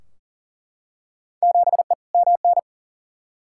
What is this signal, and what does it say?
Morse code, heard as a keyed tone, about 40 wpm, 700 Hz. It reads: 7E MN
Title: Field Day Exchanges Encore - 40wpm